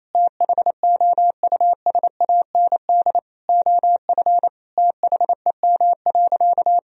Morse code, heard as a keyed tone, about 28 wpm, 700 Hz. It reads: THOUSAND OF THEM.
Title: Sentences from Top 1000 Words - Review Part 3 - 28wpm